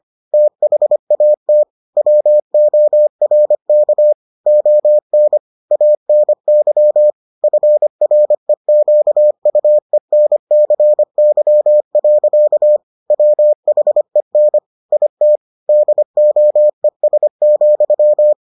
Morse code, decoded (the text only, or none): THAT WORK ON ANY FREQUENCY. WHEN IT DOES,